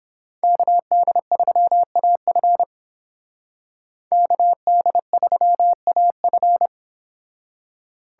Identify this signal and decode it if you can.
Morse code, heard as a keyed tone, about 31 wpm, 700 Hz. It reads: KD3AF KD3AF